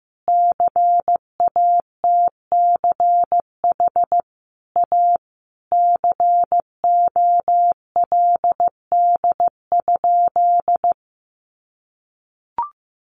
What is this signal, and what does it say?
Morse code, heard as a keyed tone, about 15 wpm, 700 Hz. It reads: CATCH A COLD?
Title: Sentences from Top 1500 Words 15wpm